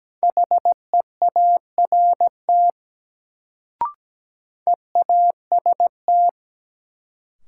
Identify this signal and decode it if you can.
Morse code, heard as a keyed tone, about 17 wpm, 700 Hz. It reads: HEART EAST
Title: Sets of 3 Words - New Word Drill - 17wpm